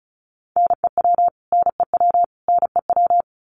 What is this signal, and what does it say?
Morse code, heard as a keyed tone, about 34 wpm, 700 Hz. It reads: NEW NEW NEW
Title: Top 200 Words Repeated 35wpm